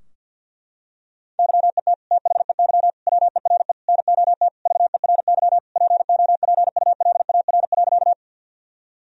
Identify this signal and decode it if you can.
Morse code, heard as a keyed tone, about 50 wpm, 700 Hz. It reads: YET THEY WERE NOT VERY POPULAR.
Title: Sentences from Top 1000 Words - Review Part 3 - 50wpm